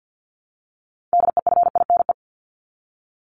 Morse code, heard as a keyed tone, about 50 wpm, 700 Hz. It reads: DEFINE
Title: ED Suffix - Top 100 Words with Base Words 50wpm